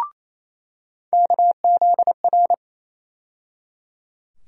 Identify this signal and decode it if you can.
Morse code, heard as a keyed tone, about 28 wpm, 700 Hz. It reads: KZR
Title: USA Call Sign Postfixes - 3 Letters 28wpm